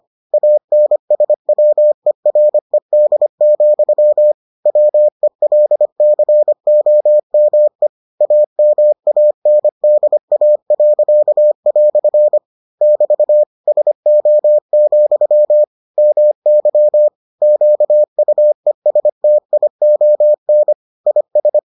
Morse code, heard as a keyed tone, about 25 wpm, 600 Hz. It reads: ANSWERED, WELCOME AMANDA." <BT> SO, MY QUESTION IS